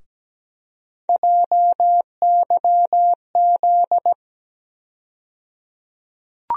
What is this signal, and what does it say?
Morse code, heard as a keyed tone, about 17 wpm, 700 Hz. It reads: JYZ
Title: USA Call Sign Postfixes - 3 Letters 17wpm